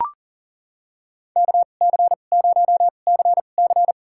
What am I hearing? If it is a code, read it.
Morse code, heard as a keyed tone, about 37 wpm, spaced as for 28 wpm, 700 Hz. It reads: KC0CC
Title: USA Call Signs - 5 Characters - 2x Character Spacing - 40wpm